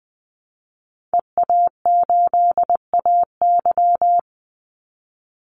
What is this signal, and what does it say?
Morse code, heard as a keyed tone, about 20 wpm, 700 Hz. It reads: EA8AY
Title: Call Signs - 20wpm